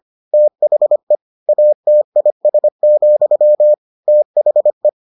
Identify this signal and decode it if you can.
Morse code, heard as a keyed tone, about 25 wpm, 600 Hz. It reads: THE ATIS, THE